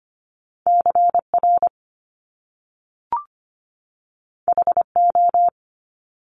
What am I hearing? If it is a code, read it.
Morse code, heard as a keyed tone, about 25 wpm, 700 Hz. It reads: CR HO
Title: Top Two Letter Combinations in Words 25wpm